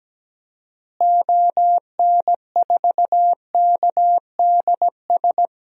Morse code, heard as a keyed tone, about 17 wpm, 700 Hz. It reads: ON4KDS
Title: DXCC-100 Call Signs - 6 Characters - Repeated 3x - 17wpm